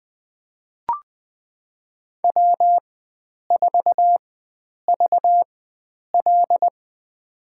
Morse code, heard as a keyed tone, about 20 wpm, spaced as for 9 wpm, 700 Hz. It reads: W4VL
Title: USA Call Signs - 4x Character Spacing - 20wpm